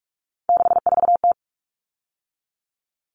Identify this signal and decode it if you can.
Morse code, heard as a keyed tone, about 44 wpm, 700 Hz. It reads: 64T
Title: Common Cut Numbers - 3-Digit 45wpm